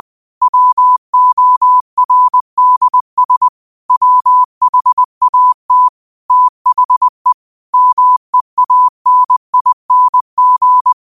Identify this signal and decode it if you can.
Morse code, heard as a keyed tone, about 20 wpm, 1000 Hz. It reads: WORDS WHAT THE MEANING
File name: qotdfiles / 20wpm